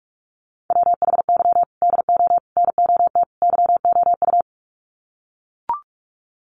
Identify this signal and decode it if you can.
Morse code, heard as a keyed tone, about 44 wpm, 700 Hz. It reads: WHY DO NOT YOU
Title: Sets of 4 Words 45wpm